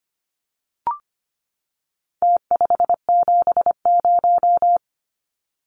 Morse code, heard as a keyed tone, about 25 wpm, 700 Hz. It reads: T570
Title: Numbers - Contest Style - 4-Digit 25wpm